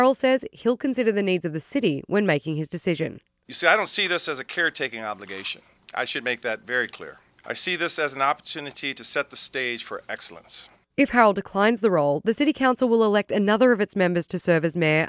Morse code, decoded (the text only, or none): none